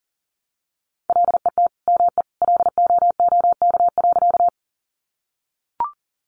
Morse code, heard as a keyed tone, about 40 wpm, 700 Hz. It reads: LET ME LOOK.
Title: Sentences from Top 300 Words 40wpm